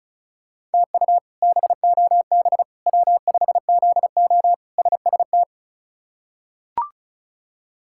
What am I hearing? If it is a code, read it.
Morse code, heard as a keyed tone, about 34 wpm, 700 Hz. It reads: TU BOB W5ZO SST
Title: K1USN SST QSO Practice - 35wpm